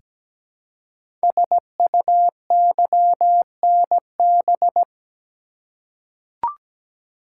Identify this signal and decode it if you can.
Morse code, heard as a keyed tone, about 17 wpm, 700 Hz. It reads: SUYNB
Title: Code Groups - 5 Letters - 17wpm